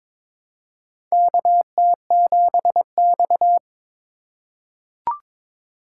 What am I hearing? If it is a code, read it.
Morse code, heard as a keyed tone, about 22 wpm, 700 Hz. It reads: KT7X
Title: USA Call Signs - 4 Characters - 3x Character Spacing - 22wpm